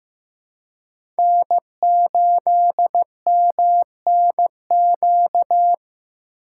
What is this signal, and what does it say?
Morse code, heard as a keyed tone, about 15 wpm, 700 Hz. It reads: N8MNQ
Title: USA Call Signs - 5 Characters - 3x Character Spacing - 15wpm